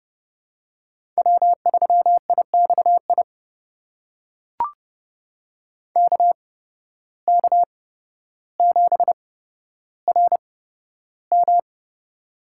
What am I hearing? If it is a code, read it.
Morse code, heard as a keyed tone, about 32 wpm, spaced as for 13 wpm, 700 Hz. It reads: W3IXI KK7RM
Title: USA Call Signs - 5 Characters - 8x Character Spacing - 30wpm